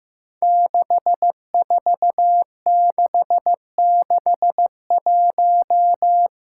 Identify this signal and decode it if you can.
Morse code, heard as a keyed tone, about 15 wpm, 700 Hz. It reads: 64661
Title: Five Challenging Letter-Number Combinations 15wpm